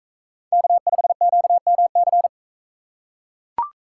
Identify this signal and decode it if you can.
Morse code, heard as a keyed tone, about 39 wpm, 700 Hz. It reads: K5QMC